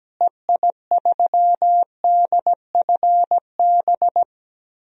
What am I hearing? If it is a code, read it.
Morse code, heard as a keyed tone, about 17 wpm, 700 Hz. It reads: EI3DFB
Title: DXCC-100 Call Signs - 6 Characters - Repeated 3x - 17wpm